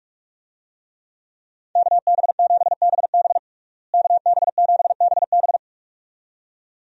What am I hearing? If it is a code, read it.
Morse code, heard as a keyed tone, about 46 wpm, 700 Hz. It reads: KB7BB KB7BB
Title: USA Call Signs - 5 Characters - Repeated 2x - 45wpm